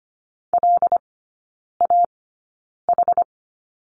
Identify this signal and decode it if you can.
Morse code, heard as a keyed tone, about 25 wpm, spaced as for 9 wpm, 700 Hz. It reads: LAH
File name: Lesson 17e - Letter u - Three Characters 25wpm